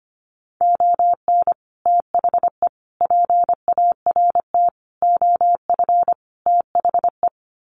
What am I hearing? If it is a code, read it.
Morse code, heard as a keyed tone, about 25 wpm, 700 Hz. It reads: ON THE PART OF THE
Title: Sets of 5 Words 25wpm